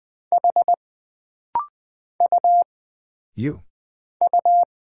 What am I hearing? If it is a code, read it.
Morse code, heard as a keyed tone, about 20 wpm, 700 Hz. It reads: H U U
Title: Single Character (Letter:Number:Punctuation:Prosign) - Rapid-Fire - 20wpm